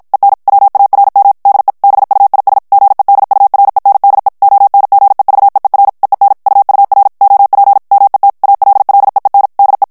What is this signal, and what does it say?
Morse code, heard as a keyed tone, about 56 wpm, 800 Hz. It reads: EN WARM DE BUIS GEDURENDE ONGEVEER EEN UUR OP MET ALLEEN DE
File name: morse mp3 / HOLLANDAIS texte clair 2 vitesse 50 mots minute